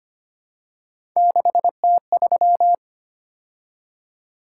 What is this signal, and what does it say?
Morse code, heard as a keyed tone, about 25 wpm, 700 Hz. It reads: 6T3
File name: Three Letter-Number Combinations - 25wpm